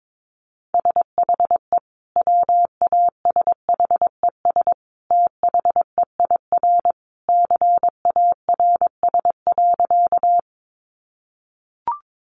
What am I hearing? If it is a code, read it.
Morse code, heard as a keyed tone, about 22 wpm, 700 Hz. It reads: SHE WASHES THEIR CARS.
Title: Easy Phrases 22wpm